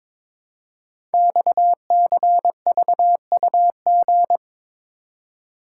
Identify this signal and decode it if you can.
Morse code, heard as a keyed tone, about 22 wpm, 700 Hz. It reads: XCVUG
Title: Code Groups - 5 Letter:Numbers - 22wpm